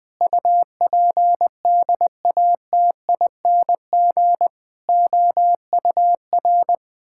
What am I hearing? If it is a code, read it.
Morse code, heard as a keyed tone, about 21 wpm, 700 Hz. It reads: UPDATING OUR
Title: Sentences with 6-Word Phrases - 20wpm